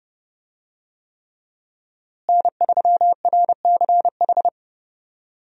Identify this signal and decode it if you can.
Morse code, heard as a keyed tone, about 32 wpm, 700 Hz. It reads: N3RCH
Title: USA Call Signs - 5 Characters - Repeated 3x - 30wpm